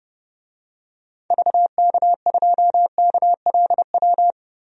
Morse code, heard as a keyed tone, about 30 wpm, 700 Hz. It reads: VK2KLW
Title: DXCC-100 Call Signs - 6 Characters - 30wpm